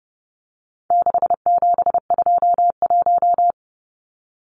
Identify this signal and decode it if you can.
Morse code, heard as a keyed tone, about 30 wpm, 700 Hz. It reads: 6721
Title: Numbers - 4-Digit 30wpm